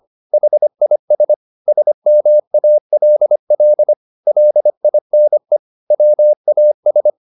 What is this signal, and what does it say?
Morse code, heard as a keyed tone, about 25 wpm, 600 Hz. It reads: HIS SMALL LINE WAS